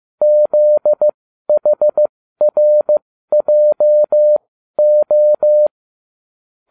Morse code, heard as a keyed tone, about 15 wpm, spaced as for 12 wpm, 600 Hz. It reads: ZHRJO